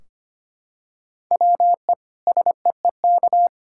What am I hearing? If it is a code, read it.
Morse code, heard as a keyed tone, about 25 wpm, 700 Hz. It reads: WE SEEK
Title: Sentences from Top 1600 Words 25wpm